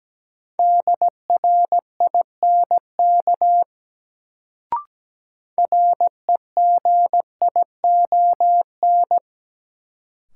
Morse code, heard as a keyed tone, about 17 wpm, 700 Hz. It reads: DRINK REGION
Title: Sentences from Top 1500 Words - New Word Drill - 17wpm